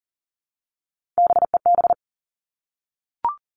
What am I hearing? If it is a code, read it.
Morse code, heard as a keyed tone, about 40 wpm, 700 Hz. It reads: BEB
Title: Three Challenging Letter-Number Combinations 40wpm